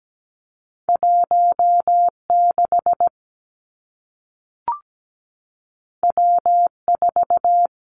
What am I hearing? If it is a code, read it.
Morse code, heard as a keyed tone, about 17 wpm, 700 Hz. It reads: 16 W4
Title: Two Challenging Letter-Number Combinations 17wpm